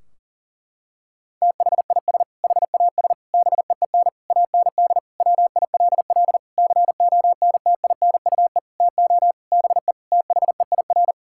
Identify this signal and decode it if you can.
Morse code, heard as a keyed tone, about 37 wpm, 700 Hz. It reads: THIS HAS BEEN AND WILL CONTINUE TO BE THEIR
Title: Sentences from Top 1600 Words 40wpm